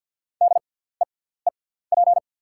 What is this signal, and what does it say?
Morse code, heard as a keyed tone, about 50 wpm, spaced as for 17 wpm, 690 Hz. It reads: DEEP
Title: 6x Character Spacing - Top 500 Words - 50wpm